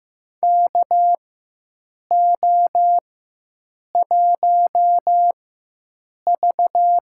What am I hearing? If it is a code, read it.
Morse code, heard as a keyed tone, about 15 wpm, spaced as for 7 wpm, 700 Hz. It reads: KO1V
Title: USA Call Signs - 4x Character Spacing - 15wpm